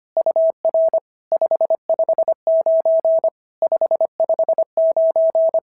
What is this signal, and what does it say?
Morse code, heard as a keyed tone, about 25 wpm, 650 Hz. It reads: UR 559 559
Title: SOTA QSO Practice 25wpm